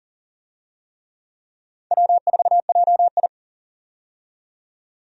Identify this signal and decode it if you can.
Morse code, heard as a keyed tone, about 40 wpm, 700 Hz. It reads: W4JI
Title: USA Call Signs - 4 Characters - 40wpm